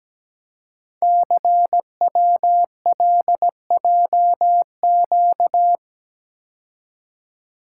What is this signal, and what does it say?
Morse code, heard as a keyed tone, about 17 wpm, 700 Hz. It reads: CWLJQ
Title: Code Groups - 5 Letters - 17wpm